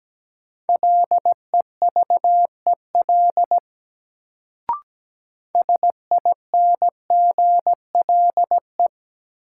Morse code, heard as a keyed tone, about 17 wpm, 700 Hz. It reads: LEVEL SINGLE
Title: Sentences from Top 700 Words - New Word Drill - 17wpm